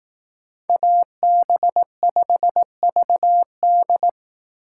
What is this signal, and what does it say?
Morse code, heard as a keyed tone, about 18 wpm, 700 Hz. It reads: AB5VD
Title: Speed Racing - USA 2x2 Call Signs 15wpm